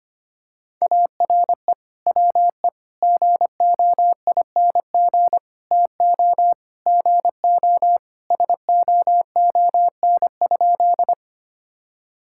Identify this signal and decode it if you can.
Morse code, heard as a keyed tone, about 25 wpm, 700 Hz. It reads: ARE WE GOING TO GO SOON?
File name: practice / 25wpm / Sentences with 3-Word Phrases - 25wpm